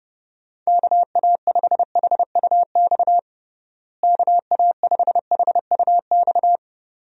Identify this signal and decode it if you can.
Morse code, heard as a keyed tone, about 30 wpm, 700 Hz. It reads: KA5HUX KA5HUX
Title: USA Call Signs - 6 Characters - Repeated 2x - 30wpm